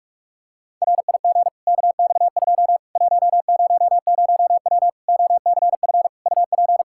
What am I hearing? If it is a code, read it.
Morse code, heard as a keyed tone, about 43 wpm, 700 Hz. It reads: RIG KX2 100W OCF UP
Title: Practice QSO - Exchange 2 - Info - 45wpm